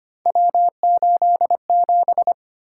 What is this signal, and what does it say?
Morse code, heard as a keyed tone, about 25 wpm, 700 Hz. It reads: W87
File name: Three Letter-Number Combinations - 25wpm